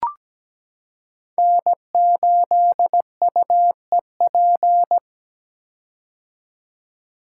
Morse code, heard as a keyed tone, about 17 wpm, 700 Hz. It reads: N8UEP
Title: Code Groups - 5 Letter:Numbers - 17wpm